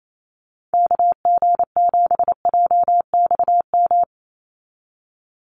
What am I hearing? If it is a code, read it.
Morse code, heard as a keyed tone, about 28 wpm, 700 Hz. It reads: KG7JXM
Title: USA Call Signs 28wpm